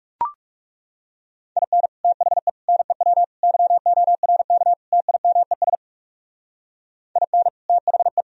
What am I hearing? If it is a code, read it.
Morse code, heard as a keyed tone, about 38 wpm, 700 Hz. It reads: IN THE NEW YORK TIMES IN THE